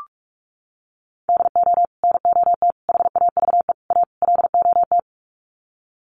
Morse code, heard as a keyed tone, about 44 wpm, 700 Hz. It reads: DO NOT HAVE A LOT
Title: Sets of 5 Words Encore 45wpm